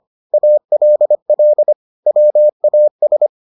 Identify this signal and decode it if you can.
Morse code, heard as a keyed tone, about 25 wpm, 600 Hz. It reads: ALL WAS